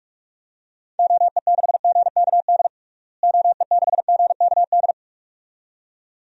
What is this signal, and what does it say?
Morse code, heard as a keyed tone, about 42 wpm, 700 Hz. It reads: OE6GKD OE6GKD
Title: DXCC-100 Call Signs - 6 Characters - Repeated 2x - 45wpm